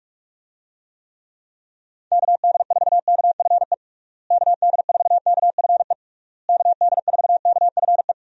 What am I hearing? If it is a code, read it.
Morse code, heard as a keyed tone, about 46 wpm, 700 Hz. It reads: KD4KFE KD4KFE KD4KFE
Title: USA Call Signs - 6 Characters - Repeated 3x - 45wpm